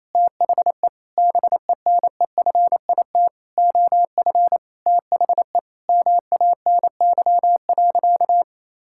Morse code, heard as a keyed tone, about 28 wpm, 700 Hz. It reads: THE BENEFIT OF THE MANY.
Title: Sentences from Top 1000 Words - Review Part 6 - 28wpm